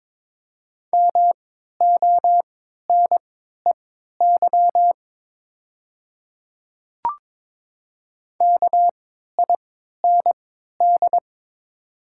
Morse code, heard as a keyed tone, about 23 wpm, spaced as for 13 wpm, 700 Hz. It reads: MONEY KIND
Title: 3x Character Spacing - Top 500 Words - 22wpm